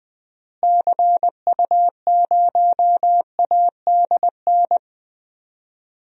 Morse code, heard as a keyed tone, about 21 wpm, 700 Hz. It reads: CU0ADN
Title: DXCC-100 Call Signs - 6 Characters - Repeated 3x - 20wpm